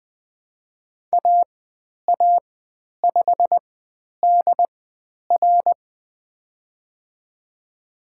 Morse code, heard as a keyed tone, about 20 wpm, spaced as for 10 wpm, 700 Hz. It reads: AA5DR
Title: 109 Lesson 18g - Letter m - Call Signs 20wpm